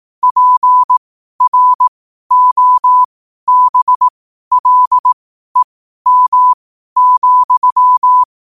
Morse code, heard as a keyed tone, about 18 wpm, spaced as for 12 wpm, 1000 Hz. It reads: PROBLEM,
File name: qotdfiles / 12wpm